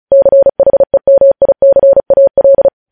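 Morse code, heard as a keyed tone, about 36 wpm, 550 Hz. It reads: CHEMICAL
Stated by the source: Morse code news headlines sent at 35, beginning with vvv.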